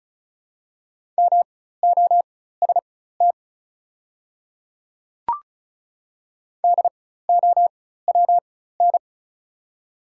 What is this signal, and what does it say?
Morse code, heard as a keyed tone, about 36 wpm, spaced as for 16 wpm, 700 Hz. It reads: MOST DOWN
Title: 4x Character Spacing - Top 200 Words - 35wpm